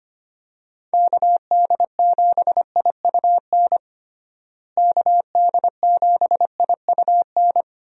Morse code, heard as a keyed tone, about 25 wpm, 700 Hz. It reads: KD7IUN KD7IUN
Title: USA Call Signs - 6 Characters - Repeated 3x - 25wpm